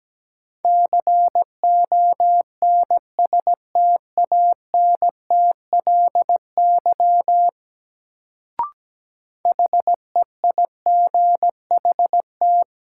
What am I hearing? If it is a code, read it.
Morse code, heard as a keyed tone, about 17 wpm, 700 Hz. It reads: CONSTANTLY HEIGHT
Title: Sentences from Top 1600 Words - New Word Drill Alt - 17wpm